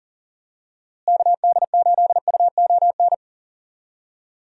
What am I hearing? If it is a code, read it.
Morse code, heard as a keyed tone, about 40 wpm, 700 Hz. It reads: KD8UON
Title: USA Call Signs - 6 Characters - Repeated 2x - 40wpm